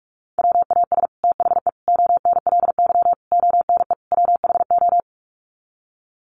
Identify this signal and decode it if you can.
Morse code, heard as a keyed tone, about 44 wpm, 700 Hz. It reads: WAS THE ONLY ONE WHO
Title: Sets of 5 Words Encore2 45wpm